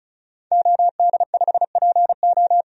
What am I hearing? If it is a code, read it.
Morse code, heard as a keyed tone, about 34 wpm, 700 Hz. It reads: OD5PO
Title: DXCC-100 Call Signs - 5 Characters - 35wpm